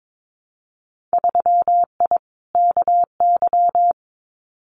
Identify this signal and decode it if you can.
Morse code, heard as a keyed tone, about 22 wpm, 700 Hz. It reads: 3I KY
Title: Winter Field Day Exchanges 22wpm